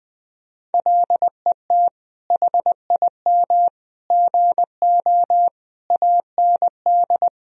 Morse code, heard as a keyed tone, about 20 wpm, 700 Hz. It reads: LET HIM GO AND
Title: Sentences from Top 1000 Words - Review Part 2 - 20wpm